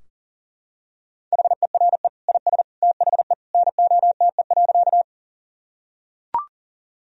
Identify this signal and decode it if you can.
Morse code, heard as a keyed tone, about 40 wpm, 700 Hz. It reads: HERE IS THE NOTE.
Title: Sentences from Top 1000 Words - Review Part 2 - 40wpm